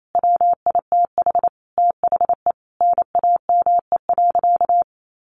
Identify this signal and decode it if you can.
Morse code, heard as a keyed tone, about 28 wpm, 700 Hz. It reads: WITH THE NAME.
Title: Sentences from Top 1100 Words 28wpm